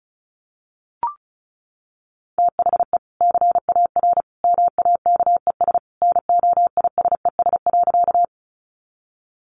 Morse code, heard as a keyed tone, about 34 wpm, 700 Hz. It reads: THE CAR MAKES NOISES.
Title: Easy Phrases 35wpm